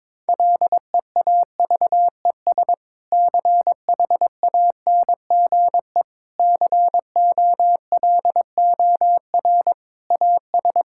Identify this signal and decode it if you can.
Morse code, heard as a keyed tone, about 21 wpm, 700 Hz. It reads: LEAVES CHANGE COLOR AS
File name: Difficult Sentences with Hints - 22wpm